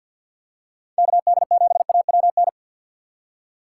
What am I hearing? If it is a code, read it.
Morse code, heard as a keyed tone, about 53 wpm, 700 Hz. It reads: KD7AWN
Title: USA Call Signs - 6 Characters - 6x Character Spacing - 50wpm